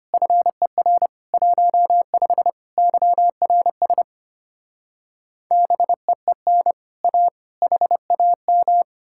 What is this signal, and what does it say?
Morse code, heard as a keyed tone, about 33 wpm, 700 Hz. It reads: FER 15 YRS BEEN A HAM